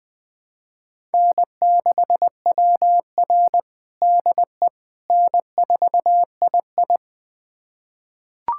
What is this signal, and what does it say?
Morse code, heard as a keyed tone, about 20 wpm, 700 Hz. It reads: N6WR DE N4II
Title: USA Callsign DE Callsign - 4 Characters - 20wpm